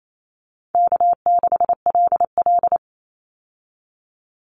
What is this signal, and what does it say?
Morse code, heard as a keyed tone, about 28 wpm, 700 Hz. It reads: K6LL
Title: Easy Call Signs Repeated 2x - 28wpm